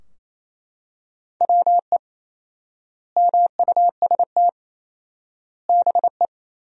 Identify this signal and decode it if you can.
Morse code, heard as a keyed tone, about 28 wpm, 700 Hz. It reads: WE MUST BE
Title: 4x Word-Spacing - Sentences from Top 500 Words - 28wpm